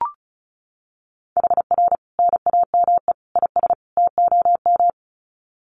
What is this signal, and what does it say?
Morse code, heard as a keyed tone, about 34 wpm, 700 Hz. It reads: HR NAME IS TOM
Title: Common QSO Phrases 35wpm